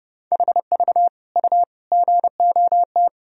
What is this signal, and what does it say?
Morse code, heard as a keyed tone, about 32 wpm, 700 Hz. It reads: HV U GOT
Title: Sentences from Top 300 Words and Abbreviations 30wpm